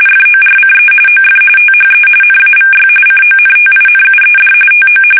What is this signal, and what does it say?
RTTY, about 45.45 baud. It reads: FORCE OF THE NAVY CONTINUED TO
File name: fsk850 / usnavy2